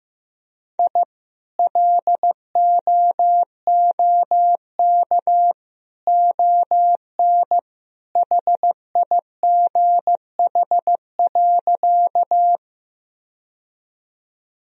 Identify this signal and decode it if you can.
Morse code, heard as a keyed tone, about 16 wpm, 700 Hz. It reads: I LOOK ON HIGH.
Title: Sentences from Top 1000 Words - Review Part 1 - 15wpm